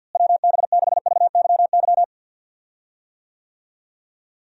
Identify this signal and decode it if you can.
Morse code, heard as a keyed tone, about 53 wpm, 700 Hz. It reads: WB6VYY
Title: USA Call Signs - 6 Characters - Repeated 2x - 50wpm